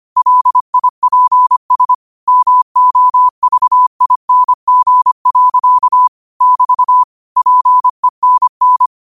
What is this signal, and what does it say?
Morse code, heard as a keyed tone, about 25 wpm, 1000 Hz. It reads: LIPS MOVING. <BT> PENN
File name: qotdfiles / 25wpm